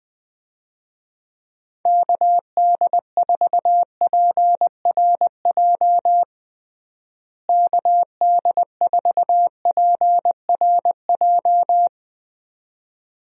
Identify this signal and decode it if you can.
Morse code, heard as a keyed tone, about 21 wpm, 700 Hz. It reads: KD4PRJ KD4PRJ
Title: USA Call Signs - 6 Characters - Repeated 2x - 20wpm